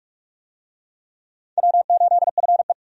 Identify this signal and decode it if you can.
Morse code, heard as a keyed tone, about 42 wpm, 700 Hz. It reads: W8FE
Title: USA Call Signs - 4 Characters - 45wpm